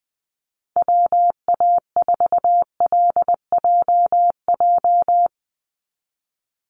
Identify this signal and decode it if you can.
Morse code, heard as a keyed tone, about 20 wpm, 700 Hz. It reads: WA4LJJ
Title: Call Signs Repeated 3x - 20wpm